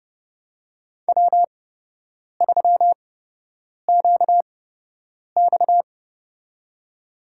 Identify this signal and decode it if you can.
Morse code, heard as a keyed tone, about 30 wpm, spaced as for 8 wpm, 700 Hz. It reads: W3QX
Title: USA Call Signs - 4 Characters - 8x Character Spacing - 30wpm